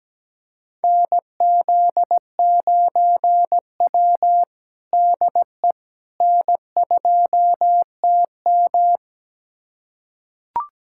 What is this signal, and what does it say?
Morse code, heard as a keyed tone, about 17 wpm, 700 Hz. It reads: NZ9W DE N2TM
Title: USA Callsign DE Callsign - 4 Characters - 17wpm